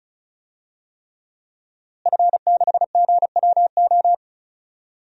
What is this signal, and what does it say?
Morse code, heard as a keyed tone, about 34 wpm, 700 Hz. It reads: F6GWO
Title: DXCC-100 Call Signs - 5 Characters - Repeated 3x - 35wpm